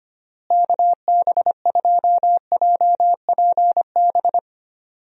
Morse code, heard as a keyed tone, about 25 wpm, 700 Hz. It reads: KB2JPB
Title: USA Call Signs - 6 Characters - Repeated 3x - 25wpm